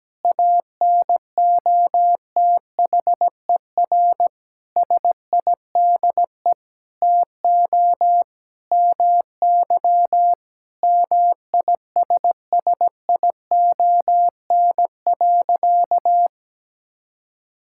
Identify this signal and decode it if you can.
Morse code, heard as a keyed tone, about 17 wpm, 700 Hz. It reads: ANOTHER SIDE TO MY MISSION.
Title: Sentences from Top 1600 Words 17wpm